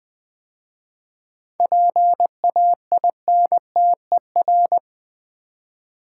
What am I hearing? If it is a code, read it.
Morse code, heard as a keyed tone, about 20 wpm, 700 Hz. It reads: PAINTER
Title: 7 Letter Words - First 3 Letters Given - 20wpm